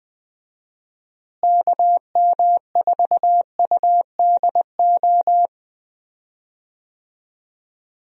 Morse code, heard as a keyed tone, about 20 wpm, 700 Hz. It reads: KM4UDO
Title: USA Call Signs - 6 Characters - Repeated 2x - 20wpm